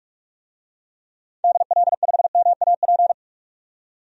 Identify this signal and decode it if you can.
Morse code, heard as a keyed tone, about 44 wpm, 690 Hz. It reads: DL5MAP
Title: DXCC-100 Call Signs - 6 Characters - 45wpm